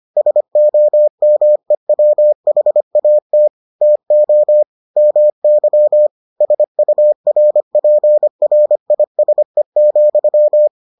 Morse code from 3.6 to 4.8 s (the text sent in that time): TO